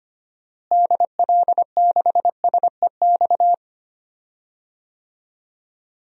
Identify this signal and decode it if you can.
Morse code, heard as a keyed tone, about 25 wpm, 700 Hz. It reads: DL6SEX
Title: DXCC-100 Call Signs - 6 Characters - Repeated 2x - 25wpm